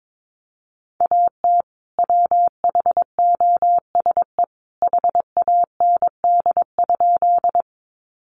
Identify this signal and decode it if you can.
Morse code, heard as a keyed tone, about 22 wpm, 700 Hz. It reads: AT WHOSE HAND?
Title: Sentences from Top 1200 Words 22wpm